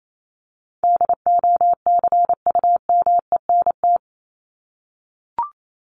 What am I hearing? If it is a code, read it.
Morse code, heard as a keyed tone, about 28 wpm, 700 Hz. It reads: DOCUMENT
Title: Top 1751-2000 Words 28wpm